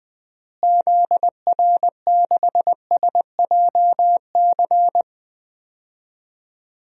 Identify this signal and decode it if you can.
Morse code, heard as a keyed tone, about 20 wpm, 700 Hz. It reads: ZR6SJC
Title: DXCC-100 Call Signs - 6 Characters - Repeated 3x - 20wpm